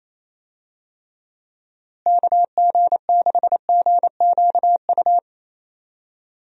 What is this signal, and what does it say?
Morse code, heard as a keyed tone, about 28 wpm, 700 Hz. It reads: KG6GQU
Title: USA Call Signs - 6 Characters - Repeated 3x - 28wpm